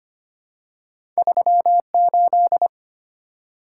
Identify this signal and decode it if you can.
Morse code, heard as a keyed tone, about 25 wpm, 700 Hz. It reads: 38
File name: Two Letter-Number Combinations - 25wpm